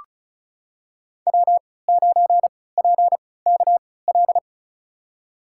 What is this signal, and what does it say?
Morse code, heard as a keyed tone, about 36 wpm, spaced as for 20 wpm, 700 Hz. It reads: W9PKL
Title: USA Call Signs - 5 Characters - 3x Character Spacing - 35wpm